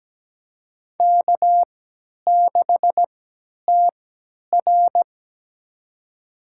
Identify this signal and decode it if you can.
Morse code, heard as a keyed tone, about 17 wpm, spaced as for 10 wpm, 700 Hz. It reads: K6TR
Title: USA Call Signs - 4 Characters - 3x Character Spacing - 17wpm